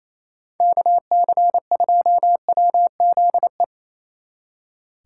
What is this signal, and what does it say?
Morse code, heard as a keyed tone, about 34 wpm, 700 Hz. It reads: KC2WZE
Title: Speed Racing - USA 2x3 Call Signs 28wpm